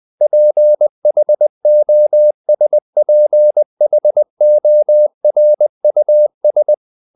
Morse code, heard as a keyed tone, about 20 wpm, 600 Hz. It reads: PHOSPHORUS
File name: the old man and the sea part 2 20wpm